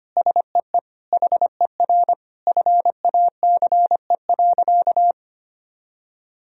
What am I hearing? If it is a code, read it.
Morse code, heard as a keyed tone, about 25 wpm, 700 Hz. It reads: SEE HER FACE.
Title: Sentences from Top 1000 Words - Review Part 2 - 25wpm